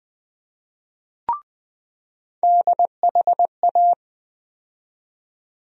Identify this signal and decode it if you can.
Morse code, heard as a keyed tone, about 20 wpm, 700 Hz. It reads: DHA
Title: USA Call Sign Postfixes - 3 Letters 20wpm